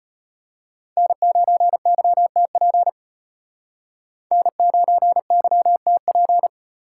Morse code, heard as a keyed tone, about 33 wpm, 700 Hz. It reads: N9YTP N9YTP